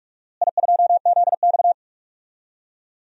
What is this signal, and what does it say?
Morse code, heard as a keyed tone, about 46 wpm, 700 Hz. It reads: I2ZX
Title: DXCC-100 Call Signs - 4 Characters - Repeated 2x - 45wpm